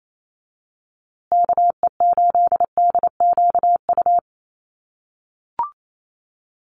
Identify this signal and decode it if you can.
Morse code, heard as a keyed tone, about 28 wpm, 700 Hz. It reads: KE8DQU
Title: USA Call Signs Repeated 3x 28wpm